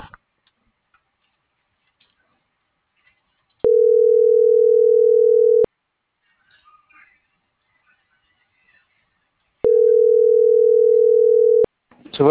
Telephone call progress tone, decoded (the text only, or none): ringback tone